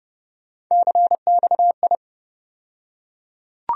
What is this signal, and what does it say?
Morse code, heard as a keyed tone, about 30 wpm, 700 Hz. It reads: CXI
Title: USA Call Sign Postfixes - 3 Letters 30wpm